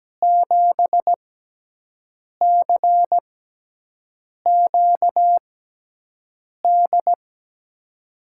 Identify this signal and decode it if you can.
Morse code, heard as a keyed tone, about 17 wpm, 700 Hz. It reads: 7 C Q D
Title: USA Call Signs - 6 Characters - 6x Character Spacing - 17wpm